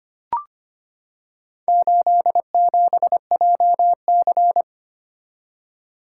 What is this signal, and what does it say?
Morse code, heard as a keyed tone, about 25 wpm, 700 Hz. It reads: 87JC
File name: practice / 25wpm / Four Letter-Number Combinations - 25wpm